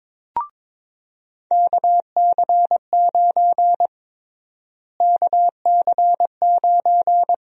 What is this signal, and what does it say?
Morse code, heard as a keyed tone, about 21 wpm, 700 Hz. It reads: KC9 KC9
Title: USA Call Sign Prefixes - Repeated 2x - 22wpm